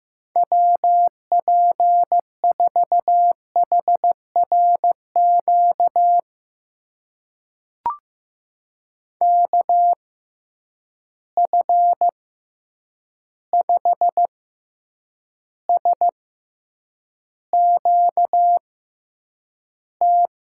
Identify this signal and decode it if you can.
Morse code, heard as a keyed tone, about 15 wpm, spaced as for 8 wpm, 700 Hz. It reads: WP4HRQ KF5SQT
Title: USA Call Signs - 6 Characters - 6x Character Spacing - 15wpm